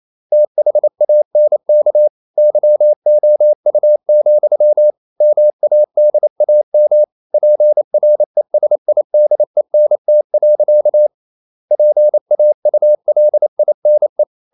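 Morse code, heard as a keyed tone, about 28 wpm, 600 Hz. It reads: THANK YOU, MADAM PRESIDENT. PAULINE
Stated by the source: Morse code practice at 28 wpm, text of Ken Follett's novel Never in English, from CW med Gnister.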